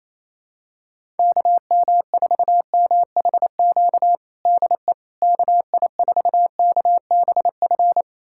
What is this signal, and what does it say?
Morse code, heard as a keyed tone, about 28 wpm, 700 Hz. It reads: KM4MHQ DE KI4KBF
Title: USA Callsign DE Callsign - 6 Characters - 28wpm